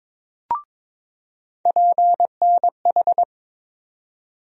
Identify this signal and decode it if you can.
Morse code, heard as a keyed tone, about 22 wpm, 700 Hz. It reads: PNH
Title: USA Call Sign Postfixes - 3 Letters 22wpm